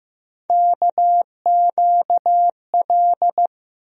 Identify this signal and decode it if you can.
Morse code, heard as a keyed tone, about 15 wpm, 700 Hz. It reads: KQL
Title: USA Call Sign Postfixes - 3 Letters 15wpm